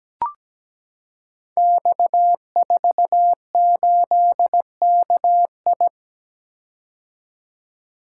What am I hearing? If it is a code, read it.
Morse code, heard as a keyed tone, about 17 wpm, 700 Hz. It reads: X48KI
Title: Code Groups - 5 Letter:Numbers - 17wpm